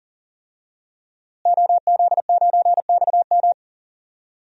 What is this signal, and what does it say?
Morse code, heard as a keyed tone, about 40 wpm, 700 Hz. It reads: OZ9XM
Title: DXCC-100 Call Signs - 5 Characters - Repeated 2x - 40wpm